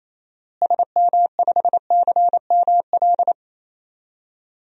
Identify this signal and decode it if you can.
Morse code, heard as a keyed tone, about 28 wpm, 700 Hz. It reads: SM5CML
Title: DXCC-100 Call Signs - 6 Characters - Repeated 3x - 28wpm